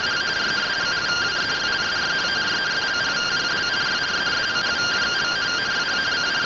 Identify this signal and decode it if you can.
Two-tone teleprinter signal, about 170 Hz shift.
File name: findings / 100 surview 0.gen-32 rtty rx